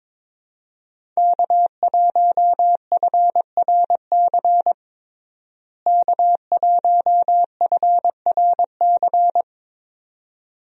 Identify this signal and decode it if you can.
Morse code, heard as a keyed tone, about 22 wpm, 700 Hz. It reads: K1FRC K1FRC
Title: USA Call Signs - 5 Characters - Repeated 2x - 22wpm